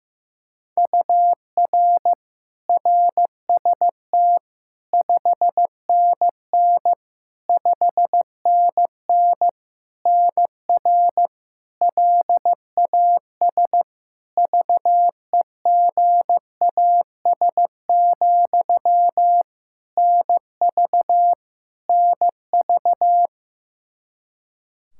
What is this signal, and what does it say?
Morse code, heard as a keyed tone, about 16 wpm, 700 Hz. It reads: UR RST 5NN 5NN NR LAS VEGAS, NV NV
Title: K1USN SST QSO Practice 15wpm